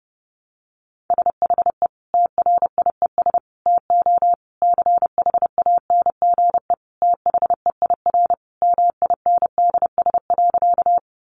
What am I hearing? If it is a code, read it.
Morse code, heard as a keyed tone, about 30 wpm, 700 Hz. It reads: SHE TRIES TO CHANGE THEIR MINDS.
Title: Easy Phrases 30wpm